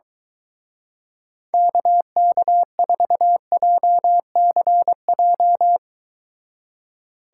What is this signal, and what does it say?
Morse code, heard as a keyed tone, about 23 wpm, 700 Hz. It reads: KK4JCJ